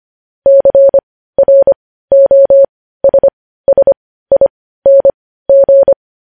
Morse code, heard as a keyed tone, about 25 wpm, spaced as for 15 wpm, 550 Hz. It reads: CROSSING
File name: podcast / News headlines-1x-25x15